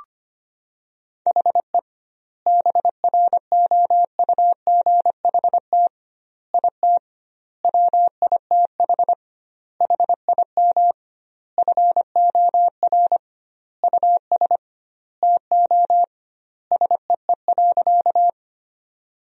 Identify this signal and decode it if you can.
Morse code, heard as a keyed tone, about 25 wpm, 700 Hz. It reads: HE BROUGHT IT WITH HIM FOR US TO SEE.
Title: 2x Word-Spacing - Sentences from Top 500 Words - 25wpm